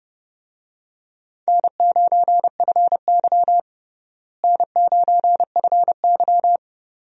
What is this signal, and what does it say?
Morse code, heard as a keyed tone, about 30 wpm, 700 Hz. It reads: N9FY N9FY
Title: USA Call Signs - 4 Characters - Repeated 2x - 30wpm